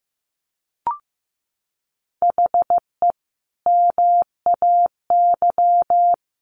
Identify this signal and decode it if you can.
Morse code, heard as a keyed tone, about 14 wpm, 700 Hz. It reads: HE MAY
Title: Sentences from Top 300 Words 15wpm